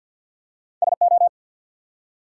Morse code, heard as a keyed tone, about 50 wpm, 700 Hz. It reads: SO
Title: Two Letter Words - 50wpm